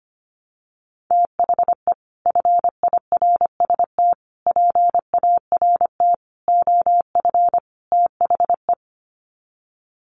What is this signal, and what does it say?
Morse code, heard as a keyed tone, about 25 wpm, 700 Hz. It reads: THE FIRST PART OF THE
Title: Sets of 5 Words Encore 25wpm